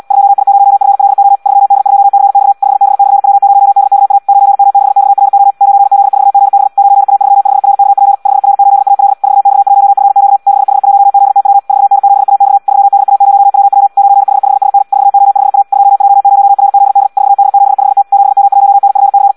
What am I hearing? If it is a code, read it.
Morse code, heard as a keyed tone, about 76 wpm, 800 Hz. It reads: 2I0FLO 3D2CX 5B/W0DKA 9M6XRO 9Z4GB 9A35RKP 5R8AL 4X1FQ 6H1ZTW 4N7AX 3DA0WW 9H5SN 4O5A 2J0RZD 4D75T 7M1NFC
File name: Audio practica CW / qrq-tx-78 2